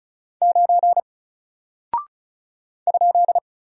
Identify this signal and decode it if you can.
Morse code, heard as a keyed tone, about 34 wpm, 700 Hz. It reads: 9 ?
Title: Single Character (Letter:Number:Punctuation:Prosign) Alt 35wpm